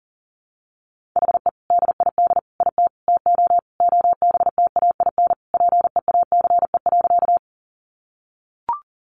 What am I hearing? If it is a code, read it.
Morse code, heard as a keyed tone, about 40 wpm, 700 Hz. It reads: HE DID IT TO OBTAIN PEACE.
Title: Sentences from Top 1500 Words 40wpm